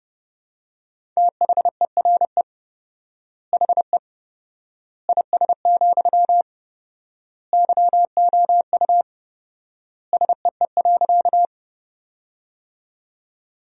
Morse code, heard as a keyed tone, about 32 wpm, 700 Hz. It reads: THERE HE IS, YOU SEE.
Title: 4x Word-Spacing - Sentences from Top 100 Words - 30wpm